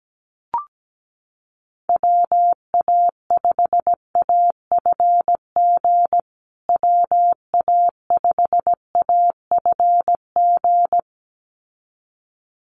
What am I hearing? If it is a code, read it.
Morse code, heard as a keyed tone, about 17 wpm, 700 Hz. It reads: WA5AFG WA5AFG
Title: USA Call Signs Repeated 2x 17wpm